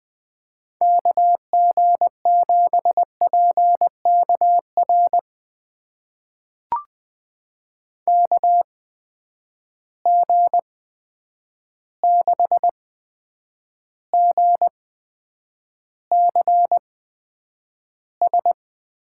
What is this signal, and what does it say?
Morse code, heard as a keyed tone, about 20 wpm, spaced as for 9 wpm, 700 Hz. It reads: KG7PKR KG6GCS
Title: USA Call Signs - 6 Characters - 8x Character Spacing - 20wpm